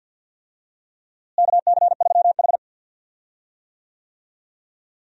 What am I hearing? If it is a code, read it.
Morse code, heard as a keyed tone, about 49 wpm, 700 Hz. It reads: KC3H
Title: USA Call Signs - 4 Characters - Repeated 2x - 50wpm